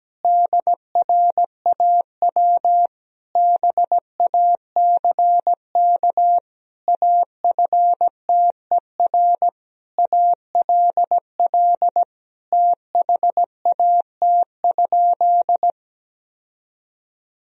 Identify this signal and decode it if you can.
Morse code, heard as a keyed tone, about 17 wpm, 700 Hz. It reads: DRAW BACK AFTER ALL THAT?
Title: Sentences from Top 1000 Words - Review Part 1 - 17wpm